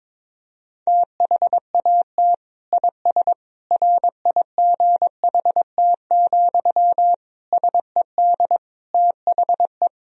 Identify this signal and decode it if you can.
Morse code, heard as a keyed tone, about 22 wpm, 700 Hz. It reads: THAT IS RIGHT, SED THE
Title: Sentences from Top 200 Words and Abbreviations 22wpm